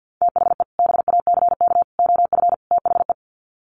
Morse code, heard as a keyed tone, about 48 wpm, 700 Hz. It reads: THE BACK OF THE
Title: Sets of 4 Words 50wpm